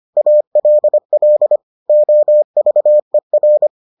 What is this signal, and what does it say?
Morse code, heard as a keyed tone, about 25 wpm, 600 Hz. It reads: ALL OVER